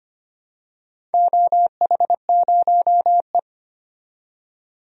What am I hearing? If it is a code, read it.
Morse code, heard as a keyed tone, about 25 wpm, 700 Hz. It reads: OH0E
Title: DXCC-100 Call Signs - 4 Characters - 25wpm